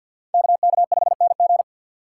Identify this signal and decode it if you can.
Morse code, heard as a keyed tone, about 63 wpm, 700 Hz. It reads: KK5NG